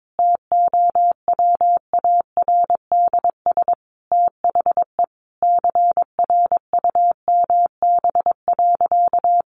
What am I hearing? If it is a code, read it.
Morse code, heard as a keyed tone, about 22 wpm, 700 Hz. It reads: TOWARDS THE CRUMB.
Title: Additional Easy Phrases 22wpm